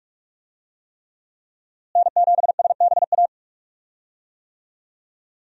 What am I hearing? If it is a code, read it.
Morse code, heard as a keyed tone, about 46 wpm, 700 Hz. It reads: N7SBA
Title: USA Call Signs - 5 Characters - Repeated 3x - 45wpm